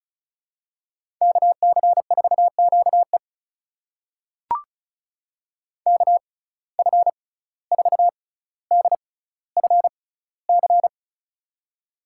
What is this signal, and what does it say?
Morse code, heard as a keyed tone, about 36 wpm, spaced as for 17 wpm, 700 Hz. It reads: KC4QE KF4DFC
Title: USA Call Signs - 6x Character Spacing - 35wpm